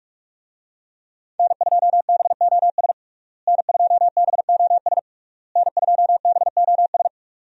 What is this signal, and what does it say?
Morse code, heard as a keyed tone, about 46 wpm, 700 Hz. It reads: N2BOS N2BOS N2BOS
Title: USA Call Signs - 5 Characters - Repeated 3x - 45wpm